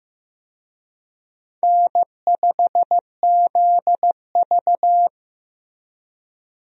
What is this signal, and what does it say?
Morse code, heard as a keyed tone, about 15 wpm, 700 Hz. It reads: N5ZV
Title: USA Call Signs - 4 Characters - Repeated 3x - 15wpm